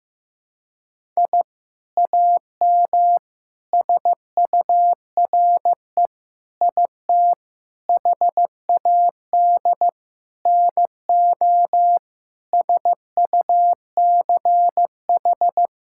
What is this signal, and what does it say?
Morse code, heard as a keyed tone, about 16 wpm, 700 Hz. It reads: I AM SURE IT HAD NO SUCH
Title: Sentences from Top 1000 Words - Review Part 6 - 15wpm